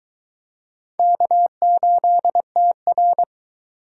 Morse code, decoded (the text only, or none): K8TR